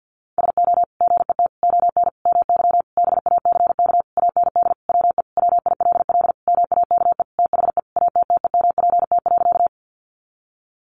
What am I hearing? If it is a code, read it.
Morse code, heard as a keyed tone, about 50 wpm, 700 Hz. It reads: SO GET ON MY BACK AND WE WILL MAKE THE ATTEMPT.
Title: Sentences from Top 1000 Words 50wpm